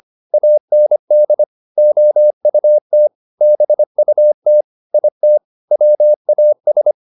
Morse code, heard as a keyed tone, about 25 wpm, 600 Hz. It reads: AND OUT BUT IT WAS